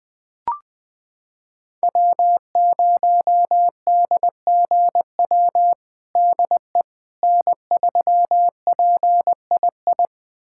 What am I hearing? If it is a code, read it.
Morse code, heard as a keyed tone, about 20 wpm, 700 Hz. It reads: W0DGW DE N3PII
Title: USA Callsign DE Callsign - 5 Characters - 20wpm